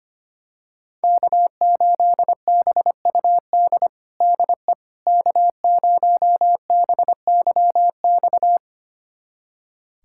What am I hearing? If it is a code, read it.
Morse code, heard as a keyed tone, about 25 wpm, 700 Hz. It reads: K8BUD DE K0BYX
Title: USA Callsign DE Callsign - 5 Characters - 25wpm